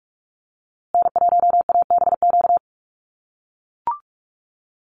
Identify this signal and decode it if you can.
Morse code, heard as a keyed tone, about 44 wpm, 700 Hz. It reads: N1ABQ
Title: USA Call Signs Repeated 3x 45wpm